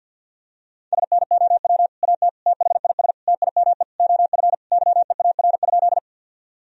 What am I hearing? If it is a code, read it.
Morse code, heard as a keyed tone, about 51 wpm, 700 Hz. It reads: SNOW AT THIS TIME OF YEAR?
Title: Sentences from Top 1000 Words - Review Part 3 - 50wpm